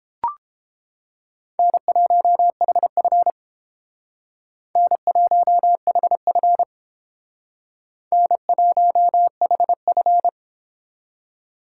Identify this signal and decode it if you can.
Morse code, heard as a keyed tone, about 30 wpm, 700 Hz. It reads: N1HF N1HF N1HF